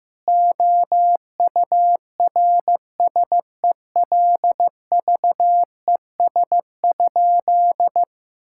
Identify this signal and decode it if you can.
Morse code, heard as a keyed tone, about 15 wpm, 700 Hz. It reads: OURSELVES?
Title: Sentences from Top 1600 Words 15wpm